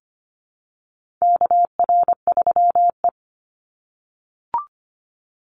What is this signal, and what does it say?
Morse code, heard as a keyed tone, about 25 wpm, 700 Hz. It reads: KR3E
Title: Call Signs - 25wpm